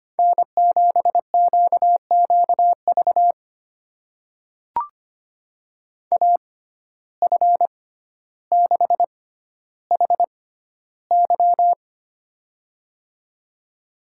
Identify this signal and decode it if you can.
Morse code, heard as a keyed tone, about 25 wpm, spaced as for 13 wpm, 700 Hz. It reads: N7QQV AF6HY
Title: USA Call Signs - 5 Characters - 6x Character Spacing - 25wpm